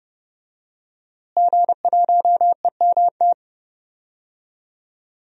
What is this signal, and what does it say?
Morse code, heard as a keyed tone, about 30 wpm, 700 Hz. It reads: G1EMT
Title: DXCC-100 Call Signs - 5 Characters - Repeated 2x - 30wpm